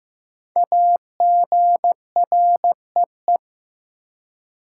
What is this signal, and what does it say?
Morse code, heard as a keyed tone, about 15 wpm, 700 Hz. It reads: AGREE
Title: Five Letter Words - 15wpm